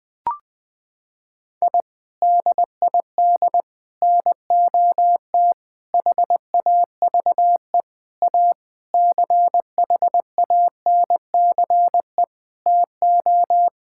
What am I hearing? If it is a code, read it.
Morse code, heard as a keyed tone, about 20 wpm, 700 Hz. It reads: I DID NOT HAVE A CHANCE TO
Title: Sentences with 5-Word Phrases - 20wpm